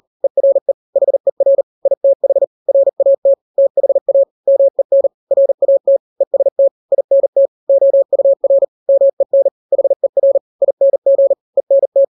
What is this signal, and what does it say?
Morse code, heard as a keyed tone, about 40 wpm, 550 Hz. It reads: ERE HER ITH RAT THA MEN RAT EST INT OUR MEN HER ING ENT
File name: n-grams / 3-grams 40 35 WPM